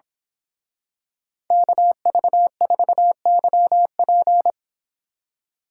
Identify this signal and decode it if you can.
Morse code, heard as a keyed tone, about 26 wpm, 700 Hz. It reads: KV4YP
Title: Speed Racing - USA Call Signs 22wpm